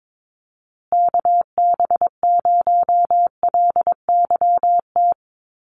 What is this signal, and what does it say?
Morse code, heard as a keyed tone, about 22 wpm, 700 Hz. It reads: KB0LYT
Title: USA Call Signs Repeated 3x 22wpm